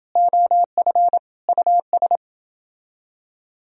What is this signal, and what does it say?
Morse code, heard as a keyed tone, about 29 wpm, 700 Hz. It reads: OF US